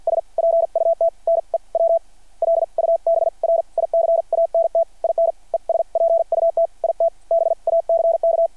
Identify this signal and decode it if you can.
Morse code, heard as a keyed tone, about 52 wpm, spaced as for 42 wpm, 660 Hz. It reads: S PUT NEW LUBRICANT IN ES PUT IT BACK
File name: archived / CRSnet